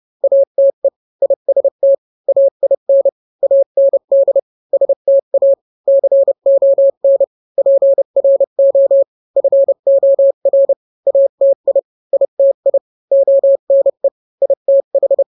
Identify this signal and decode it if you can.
Morse code, heard as a keyed tone, about 30 wpm, 550 Hz. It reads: ATE IST AIN AND STA CON PRO FOR ATI ITI ONE ITH